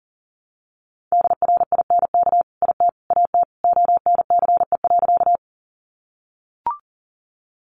Morse code, heard as a keyed tone, about 40 wpm, 700 Hz. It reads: DRINK IT AT ONCE.
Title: Sentences from Top 1500 Words 40wpm